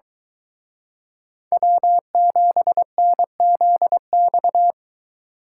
Morse code, heard as a keyed tone, about 24 wpm, 700 Hz. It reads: W7NZX